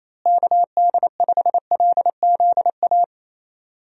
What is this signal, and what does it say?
Morse code, heard as a keyed tone, about 28 wpm, 700 Hz. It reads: KD5LZA
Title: USA Call Signs - 6 Characters - Repeated 3x - 28wpm